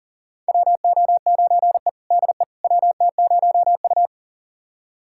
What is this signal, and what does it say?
Morse code, heard as a keyed tone, about 40 wpm, 700 Hz. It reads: WO9E DE WT0U
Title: USA Callsign DE Callsign - 4 Characters - 40wpm